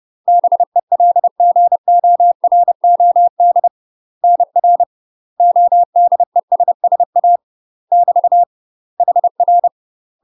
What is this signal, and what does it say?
Morse code, heard as a keyed tone, about 30 wpm, 700 Hz. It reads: BELGOROD NR ODESSA <BT> HR
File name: UK MASTER QSOs TRAINING / 30WPM / UK MASTER QSOs 0013